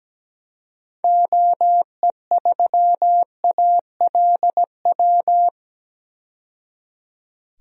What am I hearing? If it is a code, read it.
Morse code, heard as a keyed tone, about 17 wpm, 700 Hz. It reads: OE3ALW
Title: DXCC-100 Call Signs - 6 Characters - 17wpm